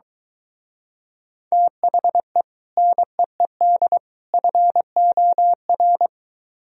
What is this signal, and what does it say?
Morse code, heard as a keyed tone, about 23 wpm, 700 Hz. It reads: THE NEED FOR